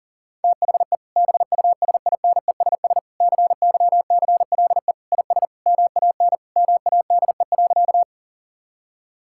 Morse code, heard as a keyed tone, about 40 wpm, 700 Hz. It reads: THE BUSINESS CYCLE IS MAN MADE.
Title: Sentences from Top 1000 Words - Review Part 6 - 40wpm